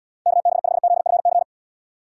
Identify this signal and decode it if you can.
Morse code, heard as a keyed tone, about 78 wpm, 700 Hz. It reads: KB5ZFC